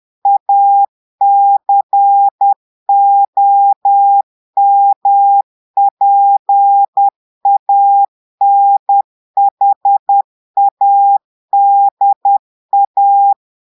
Morse code, heard as a keyed tone, about 10 wpm, 800 Hz. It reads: ACOMPANHADA
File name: portugais texte en clair 2 vitesse 10 mots minutes